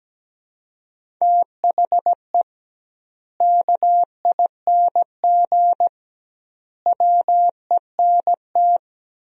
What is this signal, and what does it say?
Morse code, heard as a keyed tone, about 17 wpm, 700 Hz. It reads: THE KING WENT
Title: 2x Word-Spacing - Sentences from Top 400 Words - 17wpm